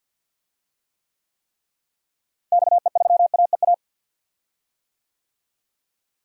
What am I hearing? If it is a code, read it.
Morse code, heard as a keyed tone, about 51 wpm, 700 Hz. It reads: XE3AEA
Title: DXCC-100 Call Signs - 6 Characters - Repeated 3x - 50wpm